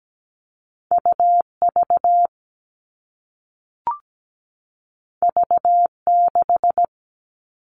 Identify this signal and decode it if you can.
Morse code, heard as a keyed tone, about 17 wpm, 700 Hz. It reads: UV V6
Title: All Cut Numbers - 2-Digit 17wpm